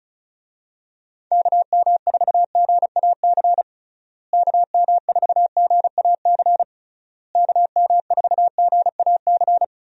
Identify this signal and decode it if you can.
Morse code, heard as a keyed tone, about 36 wpm, 700 Hz. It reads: KM4GAC KM4GAC KM4GAC
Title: USA Call Signs - 6 Characters - Repeated 3x - 35wpm